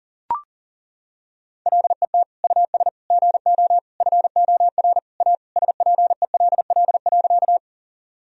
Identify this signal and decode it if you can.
Morse code, heard as a keyed tone, about 37 wpm, 700 Hz. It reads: LET US GO FOR A SPELL.
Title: Sentences from Top 1000 Words - Review Part 1 - 40wpm